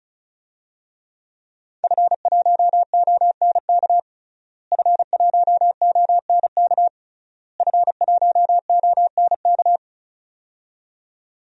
Morse code, heard as a keyed tone, about 36 wpm, 700 Hz. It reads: F1ONK F1ONK F1ONK
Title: DXCC-100 Call Signs - 5 Characters - Repeated 3x - 35wpm